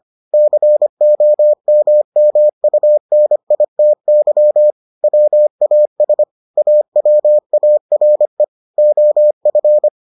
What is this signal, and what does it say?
Morse code, heard as a keyed tone, about 25 wpm, 600 Hz. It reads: COMMUNITY WAS AWARE OF